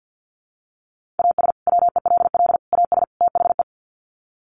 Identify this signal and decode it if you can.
Morse code, heard as a keyed tone, about 50 wpm, 700 Hz. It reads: AS WELL AS THE
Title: Sets of 4 Words 50wpm